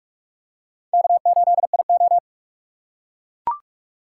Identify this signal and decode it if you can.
Morse code, heard as a keyed tone, about 44 wpm, 700 Hz. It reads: K8IO
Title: USA Call Signs - 4 Characters - 6x Character Spacing - 45wpm